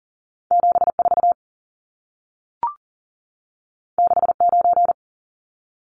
Morse code, heard as a keyed tone, about 40 wpm, 700 Hz. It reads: 74 69
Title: Numbers - 2-Digit 40wpm